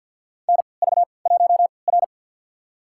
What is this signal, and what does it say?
Morse code, heard as a keyed tone, about 53 wpm, spaced as for 29 wpm, 700 Hz. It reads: NV1R
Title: USA Call Signs - 4 Characters - 3x Character Spacing - 50wpm